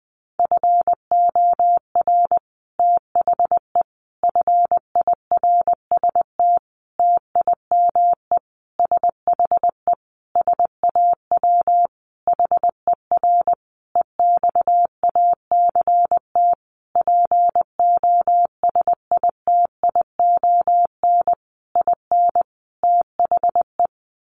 Morse code, heard as a keyed tone, about 20 wpm, 700 Hz. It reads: FOR THE FIRST TIME SHE SAW HER EXACT POSITION IN THE
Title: Sentences from Top 1300 Words 20wpm